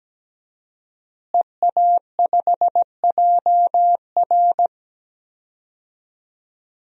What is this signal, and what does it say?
Morse code, heard as a keyed tone, about 17 wpm, 700 Hz. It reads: EA5JR
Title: DXCC-100 Call Signs - 5 Characters - Repeated 2x - 17wpm